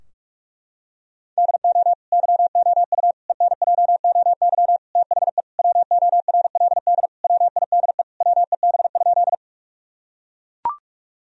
Morse code, heard as a keyed tone, about 42 wpm, 700 Hz. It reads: DO YOU ENJOY THE WORLD WIDE WEB?
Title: Sentences from Top 1000 Words - Review Part 4 - 45wpm